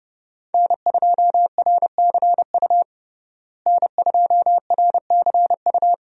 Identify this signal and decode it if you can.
Morse code, heard as a keyed tone, about 32 wpm, 700 Hz. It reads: N2RCU N2RCU
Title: USA Call Signs - 5 Characters - Repeated 2x - 30wpm